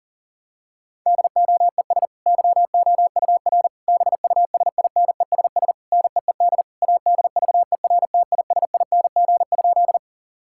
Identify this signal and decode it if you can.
Morse code, heard as a keyed tone, about 40 wpm, 700 Hz. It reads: DOES YOUR BUSINESS NEED ADVERTISING?
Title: Sentences from Top 1000 Words - Review Part 6 - 40wpm